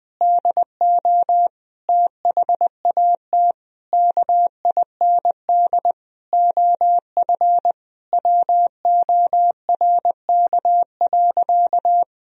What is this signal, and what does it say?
Morse code, heard as a keyed tone, about 20 wpm, 700 Hz. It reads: DO THAT KIND OF WORK.
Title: Sentences from Top 1000 Words - Review Part 2 - 20wpm